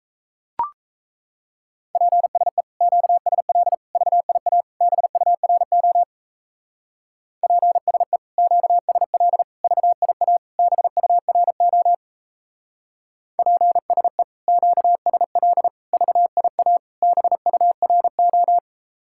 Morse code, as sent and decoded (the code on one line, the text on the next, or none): .--. ... / --.- ... .-.. / ...- .. .- / -... ..- .-. --- / .--. ... / --.- ... .-.. / ...- .. .- / -... ..- .-. --- / .--. ... / --.- ... .-.. / ...- .. .- / -... ..- .-. ---
PSE QSL VIA BURO PSE QSL VIA BURO PSE QSL VIA BURO